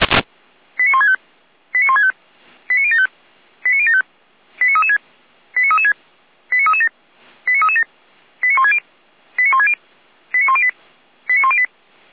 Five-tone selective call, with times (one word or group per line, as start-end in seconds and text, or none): 0.8-1.2 89165
1.7-2.1 89165
2.7-3.1 89075
3.7-4.0 89075
4.6-5.0 89336
5.6-5.9 89336
6.5-6.9 89337
7.5-7.8 89337
8.4-8.8 89160
9.4-9.7 89160
10.3-10.7 89108
11.3-11.7 89108